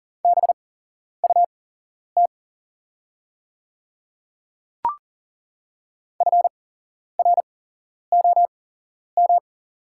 Morse code, heard as a keyed tone, about 40 wpm, spaced as for 11 wpm, 700 Hz. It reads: BUT FROM
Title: 8x Character Spacing - Top 100 Words - 40wpm